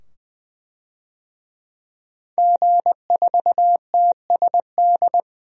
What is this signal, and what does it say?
Morse code, heard as a keyed tone, about 20 wpm, 700 Hz. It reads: G4TSD
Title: DXCC-100 Call Signs - 5 Characters - 20wpm